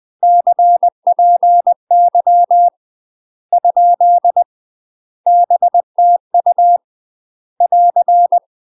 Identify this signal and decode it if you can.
Morse code, heard as a keyed tone, about 20 wpm, 700 Hz. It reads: CPY ? BTU <AR>
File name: UK MASTER QSOs TRAINING / 20WPM / UK MASTER QSOs 0002